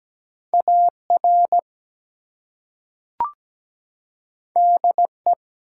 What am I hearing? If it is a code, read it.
Morse code, heard as a keyed tone, about 17 wpm, 700 Hz. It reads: AR DE
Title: US States and Canadian Provinces 17wpm